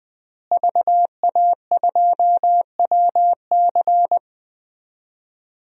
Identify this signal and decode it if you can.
Morse code, heard as a keyed tone, about 21 wpm, 700 Hz. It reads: VA2WC
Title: DXCC-100 Call Signs - 5 Characters - Repeated 3x - 20wpm